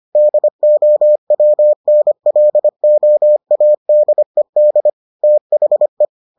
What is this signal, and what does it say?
Morse code, heard as a keyed tone, about 25 wpm, 600 Hz. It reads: DOWNLOADED THE